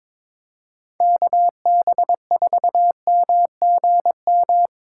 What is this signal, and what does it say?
Morse code, heard as a keyed tone, about 22 wpm, 700 Hz. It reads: KB4MGM
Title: USA Call Signs - 6 Characters - 2x Character Spacing - 22wpm